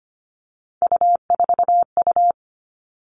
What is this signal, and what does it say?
Morse code, heard as a keyed tone, about 25 wpm, 700 Hz. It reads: U4U
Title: Three Challenging Letter-Number Combinations 25wpm